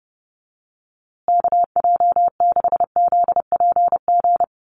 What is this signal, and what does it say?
Morse code, heard as a keyed tone, about 30 wpm, 700 Hz. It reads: KJ6ZPG
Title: USA Call Signs Repeated 3x 30wpm